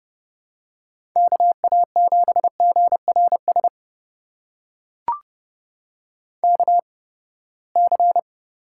Morse code, heard as a keyed tone, about 30 wpm, spaced as for 21 wpm, 700 Hz. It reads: KA7GRS KC
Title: USA Call Signs - 6 Characters - 8x Character Spacing - 30wpm